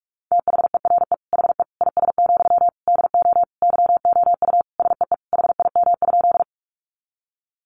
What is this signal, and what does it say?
Morse code, heard as a keyed tone, about 44 wpm, 700 Hz. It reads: THERE HE IS, DO YOU SEE HIM?
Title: Sentences from Top 100 Words 45wpm